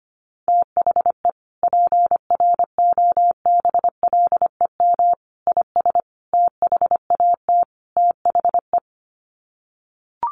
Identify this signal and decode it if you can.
Morse code, heard as a keyed tone, about 25 wpm, 700 Hz. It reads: THE PROBLEM IS THAT THE
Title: Sets of 5 Words Encore2 25wpm